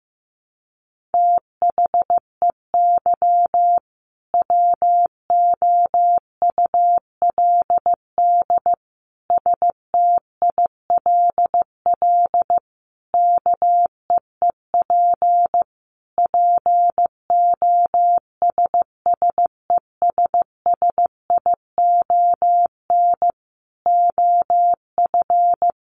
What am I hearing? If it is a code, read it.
Morse code, heard as a keyed tone, about 14 wpm, 700 Hz. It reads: THEY WOULD STILL KEEP POSSESSION OF
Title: Sentences from Top 1400 Words 15wpm